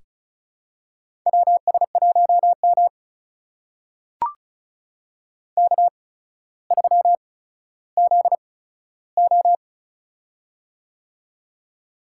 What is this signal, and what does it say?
Morse code, heard as a keyed tone, about 34 wpm, spaced as for 15 wpm, 700 Hz. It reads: WS1M K3ZO
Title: USA Call Signs - 4 Characters - 8x Character Spacing - 35wpm